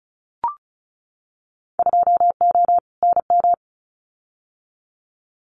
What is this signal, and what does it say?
Morse code, heard as a keyed tone, about 34 wpm, 700 Hz. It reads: 2O NM
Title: Winter Field Day Exchanges 35wpm